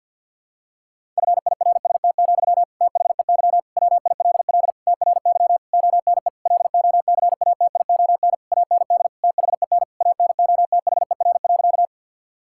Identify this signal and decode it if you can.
Morse code, heard as a keyed tone, about 52 wpm, 700 Hz. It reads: FIRST, THEY WILL TRY ONE LOCATION AND THEN ANOTHER.
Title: Sentences from Top 1000 Words - Review Part 6 - 50wpm